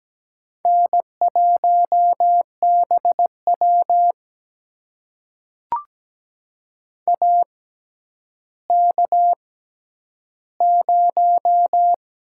Morse code, heard as a keyed tone, about 17 wpm, spaced as for 10 wpm, 700 Hz. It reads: N1BW AK0
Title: USA Call Signs - 4 Characters - 6x Character Spacing - 17wpm